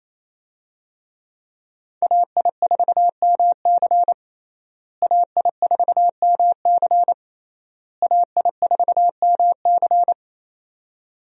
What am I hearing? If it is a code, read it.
Morse code, heard as a keyed tone, about 28 wpm, 700 Hz. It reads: AI4MC AI4MC AI4MC
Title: USA Call Signs - 5 Characters - Repeated 3x - 28wpm